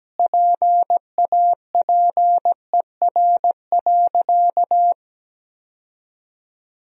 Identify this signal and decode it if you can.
Morse code, heard as a keyed tone, about 17 wpm, 700 Hz. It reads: PAPER.
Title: 2x Word-Spacing - Sentences from Top 500 Words - 17wpm